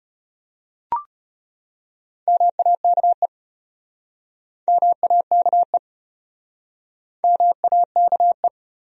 Morse code, heard as a keyed tone, about 34 wpm, 700 Hz. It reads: MAKE MAKE MAKE